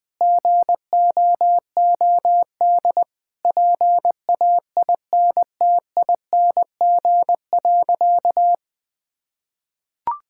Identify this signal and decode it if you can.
Morse code, heard as a keyed tone, about 20 wpm, 700 Hz. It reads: GOOD PAINTING.
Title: Sentences from Top 1000 Words - Review Part 6 - 20wpm